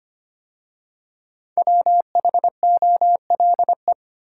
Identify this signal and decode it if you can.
Morse code, heard as a keyed tone, about 25 wpm, 700 Hz. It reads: WHOLE
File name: 5 Letter Words - First 3 Letters Given - 25wpm